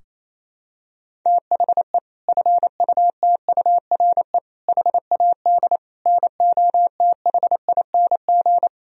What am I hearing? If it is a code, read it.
Morse code, heard as a keyed tone, about 28 wpm, 700 Hz. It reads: THE FUTURE HAD NOTHING
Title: Sentences from Top 1000 Words - Review Part 4 - 28wpm